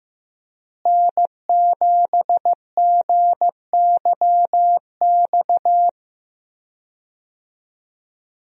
Morse code, heard as a keyed tone, about 15 wpm, 700 Hz. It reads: N7GYX
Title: USA Call Signs - 5 Characters - Repeated 3x - 15wpm